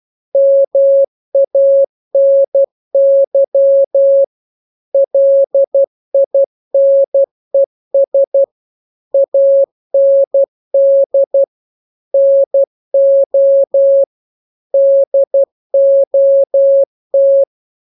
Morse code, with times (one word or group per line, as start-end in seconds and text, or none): 0.4-4.2 MANY
4.9-8.4 LINES
9.1-11.4 AND
12.1-14.0 NO
14.7-17.4 DOT